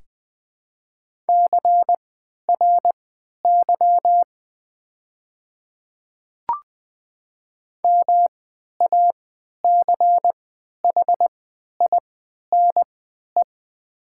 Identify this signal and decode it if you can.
Morse code, heard as a keyed tone, about 20 wpm, spaced as for 11 wpm, 700 Hz. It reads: CRY MACHINE
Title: 3x Character Spacing - Top 500 Words - 20wpm